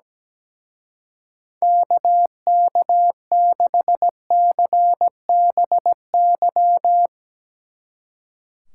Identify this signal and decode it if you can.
Morse code, heard as a keyed tone, about 17 wpm, 700 Hz. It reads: KK6CBY
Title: Speed Racing - USA 2x3 Call Signs 17wpm